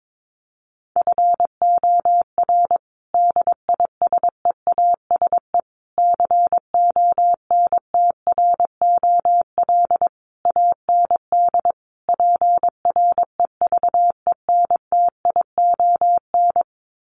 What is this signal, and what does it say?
Morse code, heard as a keyed tone, about 22 wpm, 700 Hz. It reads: FOR DISEASE CONTROL AND PREVENTION
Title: Sets of 5 Words Encore2 22wpm